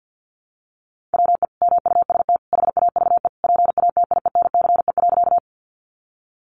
Morse code, heard as a keyed tone, about 48 wpm, 700 Hz. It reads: WE MUST HAVE PATIENCE.
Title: Sentences from Top 1300 Words 50wpm